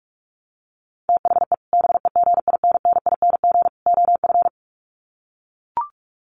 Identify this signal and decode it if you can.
Morse code, heard as a keyed tone, about 44 wpm, 700 Hz. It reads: THE BEGINNING OF
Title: Sets of 3 Words 45wpm